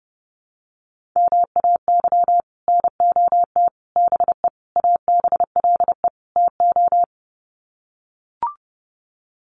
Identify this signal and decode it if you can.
Morse code, heard as a keyed tone, about 30 wpm, 700 Hz. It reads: MAY NOT BE ABLE TO
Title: Sets of 5 Words 30wpm